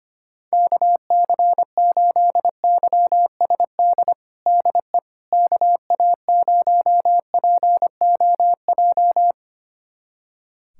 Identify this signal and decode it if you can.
Morse code, heard as a keyed tone, about 25 wpm, 700 Hz. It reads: KC8YSD DE KA0POJ
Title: USA Callsign DE Callsign - 6 Characters - 25wpm